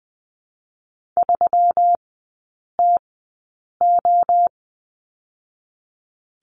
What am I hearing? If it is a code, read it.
Morse code, heard as a keyed tone, about 20 wpm, 700 Hz. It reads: 3 T O
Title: 145 Lesson 24e - Letter y - Three Characters 20wpm